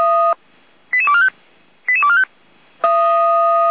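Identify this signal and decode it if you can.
Five-tone selective call (ZVEI), heard as quick runs of five tones, tones of about 70 ms. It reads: 88245 88245
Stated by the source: ZVEI five-tone call received on SDR.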